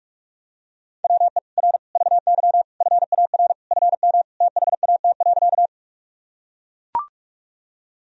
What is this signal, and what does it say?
Morse code, heard as a keyed tone, about 42 wpm, 700 Hz. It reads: WE R VY FAR FM THAT.
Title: Sentences from Top 300 Words and Abbreviations 45wpm